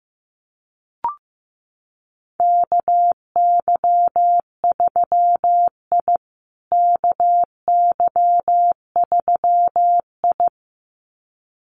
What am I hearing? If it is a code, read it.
Morse code, heard as a keyed tone, about 14 wpm, 700 Hz. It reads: KY3I KY3I
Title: USA Call Signs Repeated 2x 15wpm